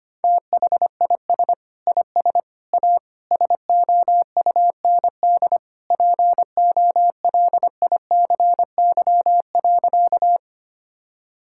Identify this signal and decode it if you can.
Morse code, heard as a keyed tone, about 25 wpm, 700 Hz. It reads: THIS IS A SOUND POLICY.
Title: Sentences from Top 1000 Words - Review Part 5 - 25wpm